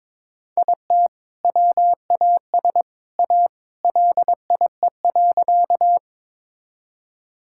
Morse code, heard as a keyed tone, about 21 wpm, 700 Hz. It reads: IT WAS A LIE.
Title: Sentences from Top 1600 Words 22wpm